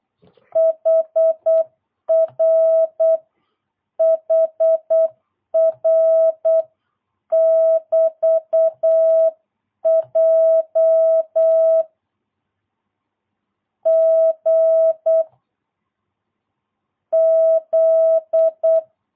Morse code, heard as a keyed tone, about 8 wpm, 650 Hz. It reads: HR HR <BT> J G Z
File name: jh8cbh / 2022cw